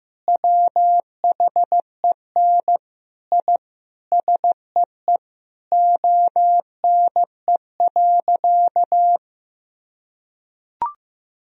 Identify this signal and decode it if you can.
Morse code, heard as a keyed tone, about 15 wpm, 700 Hz. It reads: WHEN I SEE ONE.
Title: Sentences from Top 1000 Words - Review Part 3 - 15wpm